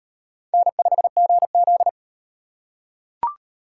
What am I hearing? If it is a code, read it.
Morse code, heard as a keyed tone, about 38 wpm, 700 Hz. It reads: N5GZ